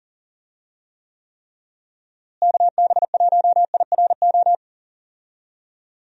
Morse code, heard as a keyed tone, about 40 wpm, 700 Hz. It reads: KB1IRO
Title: USA Call Signs - 6 Characters - Repeated 3x - 40wpm